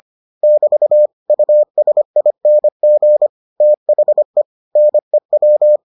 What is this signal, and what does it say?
Morse code, heard as a keyed tone, about 25 wpm, 600 Hz. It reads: <BT> USING THE NEW